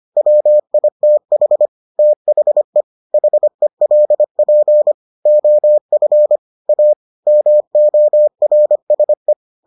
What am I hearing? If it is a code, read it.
Morse code, heard as a keyed tone, about 25 wpm, 600 Hz. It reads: WITH THE HELP OF A MORSE